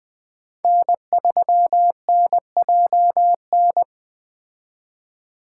Morse code, heard as a keyed tone, about 20 wpm, 700 Hz. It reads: N3NJN
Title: Speed Racing - USA 1x3 Call Signs 17wpm